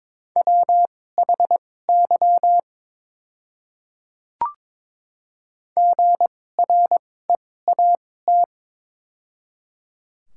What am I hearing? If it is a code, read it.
Morse code, heard as a keyed tone, about 22 wpm, spaced as for 16 wpm, 700 Hz. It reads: WHY GREAT
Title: 2x Character Spacing - Top 200 Words - 22wpm